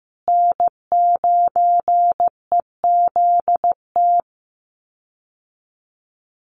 Morse code, heard as a keyed tone, about 15 wpm, 700 Hz. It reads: N9EZT
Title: USA Call Signs Repeated 3x 15wpm